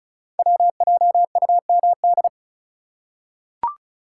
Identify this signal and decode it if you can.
Morse code, heard as a keyed tone, about 36 wpm, 690 Hz. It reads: WJUMD
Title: Code Groups - 5 Letters - 35wpm